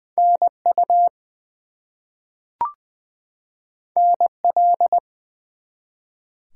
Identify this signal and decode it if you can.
Morse code, heard as a keyed tone, about 20 wpm, 700 Hz. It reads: NU NL
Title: US States and Canadian Provinces 20wpm